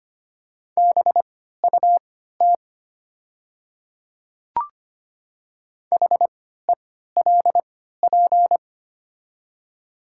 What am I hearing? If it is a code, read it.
Morse code, heard as a keyed tone, about 25 wpm, spaced as for 14 wpm, 700 Hz. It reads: BUT HELP
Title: 3x Character Spacing - Top 200 Words - 25wpm